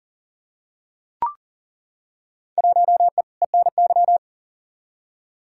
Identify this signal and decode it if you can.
Morse code, heard as a keyed tone, about 39 wpm, 700 Hz. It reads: 1E ENY
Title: Field Day Exchanges - 40wpm